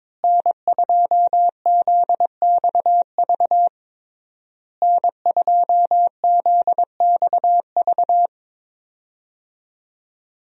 Morse code, heard as a keyed tone, about 22 wpm, 700 Hz. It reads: N2ZXV N2ZXV
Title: USA Call Signs - 5 Characters - Repeated 3x - 22wpm